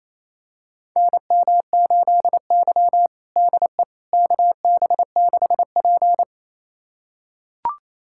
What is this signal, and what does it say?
Morse code, heard as a keyed tone, about 28 wpm, 700 Hz. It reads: NM8Y DE KB6P
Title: USA Callsign DE Callsign - 4 Characters - 28wpm